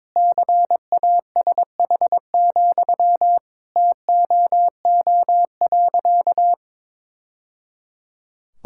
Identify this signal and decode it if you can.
Morse code, heard as a keyed tone, about 22 wpm, 700 Hz. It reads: CASH, TOO.
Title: Sentences from Top 1000 Words - Review Part 5 - 22wpm